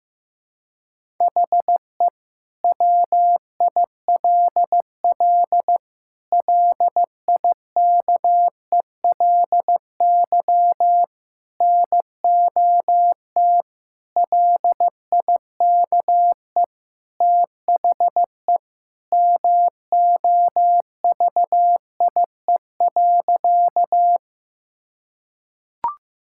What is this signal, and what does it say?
Morse code, heard as a keyed tone, about 15 wpm, 700 Hz. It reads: HE WILL LIKELY NOT LIKE THE MOVIE.
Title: Sentences from Top 1000 Words - Review Part 5 - 15wpm